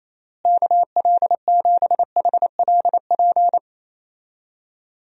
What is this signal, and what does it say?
Morse code, heard as a keyed tone, about 27 wpm, 700 Hz. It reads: KL7HLP
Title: DXCC-100 Call Signs - 6 Characters - Repeated 3x - 28wpm